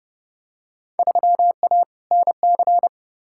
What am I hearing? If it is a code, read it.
Morse code, heard as a keyed tone, about 30 wpm, 700 Hz. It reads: 3A NC
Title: Field Day Exchanges - 30wpm